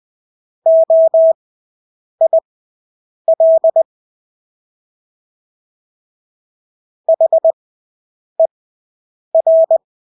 Morse code, heard as a keyed tone, about 20 wpm, spaced as for 8 wpm, 650 Hz. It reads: OIL HER
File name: Engelske ord 20/8 wpm